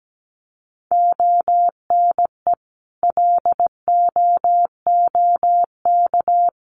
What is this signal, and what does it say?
Morse code, heard as a keyed tone, about 17 wpm, 700 Hz. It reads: ONE LOOK
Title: Sentences from Top 1400 Words 17wpm